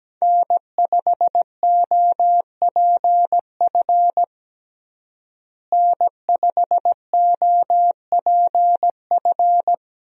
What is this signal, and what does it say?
Morse code, heard as a keyed tone, about 17 wpm, 700 Hz. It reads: N5OPF N5OPF
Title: USA Call Signs - 5 Characters - Repeated 2x - 17wpm